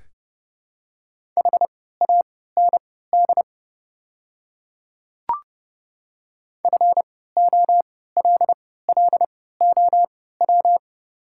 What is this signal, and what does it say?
Morse code, heard as a keyed tone, about 30 wpm, spaced as for 17 wpm, 700 Hz. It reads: HAND FOLLOW
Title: 3x Character Spacing - Top 200 Words - 30wpm